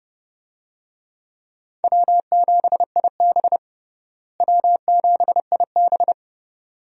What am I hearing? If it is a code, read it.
Morse code, heard as a keyed tone, about 30 wpm, 700 Hz. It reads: W7IB W7IB
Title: USA Call Signs - 4 Characters - Repeated 2x - 30wpm